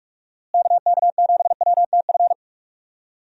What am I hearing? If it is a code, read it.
Morse code, heard as a keyed tone, about 46 wpm, 700 Hz. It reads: KK7WTF
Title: USA Call Signs - 6 Characters - Repeated 2x - 45wpm